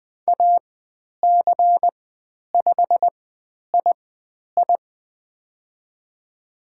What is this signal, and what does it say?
Morse code, heard as a keyed tone, about 20 wpm, spaced as for 10 wpm, 700 Hz. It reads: AC5II
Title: 102 Lesson 17g - Letter u - Call Signs 20wpm